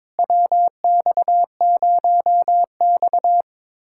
Morse code, heard as a keyed tone, about 22 wpm, 700 Hz. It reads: WX0X
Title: USA Callsign DE Callsign - 4 Characters - 22wpm